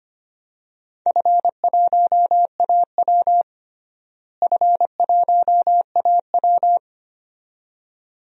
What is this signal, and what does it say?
Morse code, heard as a keyed tone, about 25 wpm, 700 Hz. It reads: F1AW F1AW
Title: DXCC-100 Call Signs - 4 Characters - Repeated 2x - 25wpm